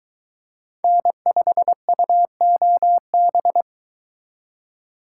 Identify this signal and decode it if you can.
Morse code, heard as a keyed tone, about 22 wpm, 700 Hz. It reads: N5UOB